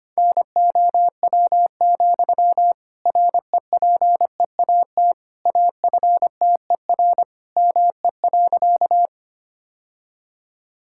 Morse code, heard as a keyed tone, about 25 wpm, 700 Hz. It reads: NOW, REPEAT AFTER ME.
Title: Sentences from Top 1600 Words 25wpm